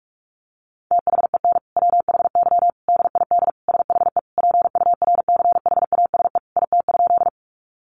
Morse code, heard as a keyed tone, about 44 wpm, 700 Hz. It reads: THEN WHY DID SHE PURCHASE IT?
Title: Sentences from Top 800 Words 45wpm